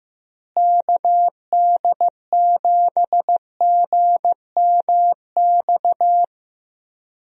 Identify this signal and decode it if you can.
Morse code, heard as a keyed tone, about 15 wpm, 700 Hz. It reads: KD7GMX
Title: USA Call Signs - 6 Characters - Repeated 3x - 15wpm